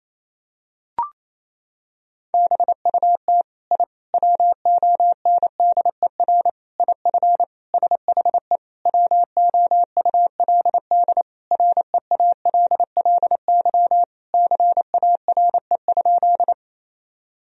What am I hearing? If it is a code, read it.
Morse code, heard as a keyed tone, about 28 wpm, 700 Hz. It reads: BUT I WONDER IF SHE WOULD REALLY CARE?
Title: Sentences from Top 1000 Words - Review Part 3 - 28wpm